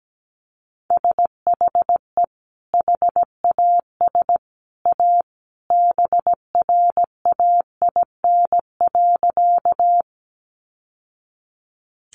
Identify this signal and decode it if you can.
Morse code, heard as a keyed tone, about 17 wpm, 700 Hz. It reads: SHE HAS A BRAIN.
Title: Sentences from Top 1100 Words 17wpm